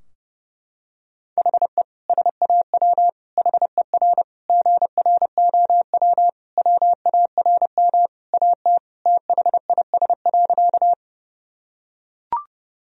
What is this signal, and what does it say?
Morse code, heard as a keyed tone, about 30 wpm, 700 Hz. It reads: HE SAW HER GROW WARM AT THIS.
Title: Sentences from Top 1000 Words - Review Part 2 - 30wpm